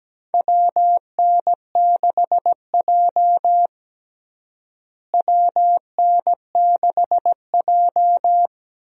Morse code, heard as a keyed tone, about 17 wpm, 700 Hz. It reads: WN6J WN6J
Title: USA Call Signs - 4 Characters - Repeated 2x - 17wpm